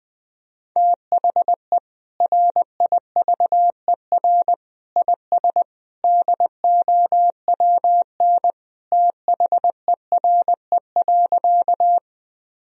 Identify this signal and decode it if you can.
Morse code, heard as a keyed tone, about 20 wpm, 700 Hz. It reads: THE RIVER IS DOWN THERE.
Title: Sentences from Top 1000 Words - Review Part 1 - 20wpm